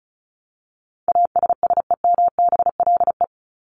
Morse code, heard as a keyed tone, about 34 wpm, 700 Hz. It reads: ASSEMBLE
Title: LY Suffix - Top 100 Words with Base Words 35wpm